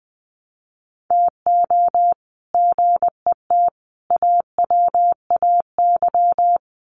Morse code, heard as a keyed tone, about 20 wpm, 700 Hz. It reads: TO GET AWAY
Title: Sets of 5 Words Encore2 20wpm